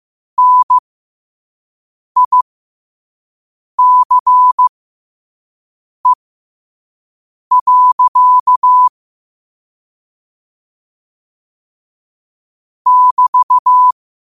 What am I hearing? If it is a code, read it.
Morse code, heard as a keyed tone, about 15 wpm, spaced as for 5 wpm, 1000 Hz. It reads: NICE. <BT>
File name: qotdfiles / 5wpm